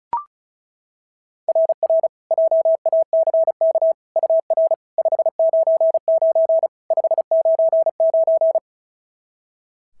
Morse code, heard as a keyed tone, about 35 wpm, 690 Hz. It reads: RR JACK UR 599 599
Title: SOTA QSO Practice 35wpm